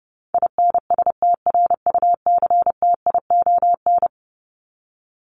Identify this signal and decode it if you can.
Morse code, heard as a keyed tone, about 30 wpm, 700 Hz. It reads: INSTRUCTION
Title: Top 1751-2000 Words 30wpm